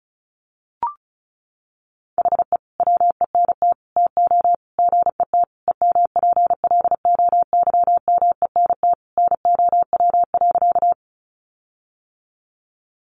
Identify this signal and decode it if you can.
Morse code, heard as a keyed tone, about 34 wpm, 700 Hz. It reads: HE WENT TO GET EMPLOYMENT NOW.
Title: Sentences from Top 1500 Words 35wpm